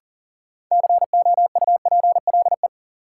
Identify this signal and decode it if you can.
Morse code, heard as a keyed tone, about 40 wpm, 700 Hz. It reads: COUPLE
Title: Sets of 2 Words - New Word Drill - 40wpm